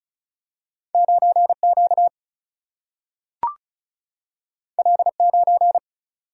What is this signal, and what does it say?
Morse code, heard as a keyed tone, about 36 wpm, 700 Hz. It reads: 9 Q L 9
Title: Single-Letter Single-Number 35wpm